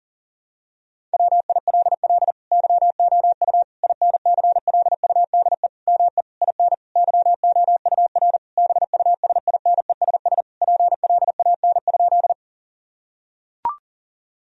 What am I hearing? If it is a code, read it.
Morse code, heard as a keyed tone, about 37 wpm, 700 Hz. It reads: WILL YOU INCLUDE ME IN YOUR BUSINESS PLAN?
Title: Sentences from Top 1000 Words - Review Part 3 - 40wpm